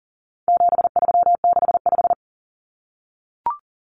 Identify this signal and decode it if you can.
Morse code, heard as a keyed tone, about 39 wpm, 700 Hz. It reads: 7365
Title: Common Cut Numbers - 4-Digit 40wpm